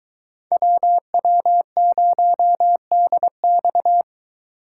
Morse code, heard as a keyed tone, about 23 wpm, 700 Hz. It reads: WW0DX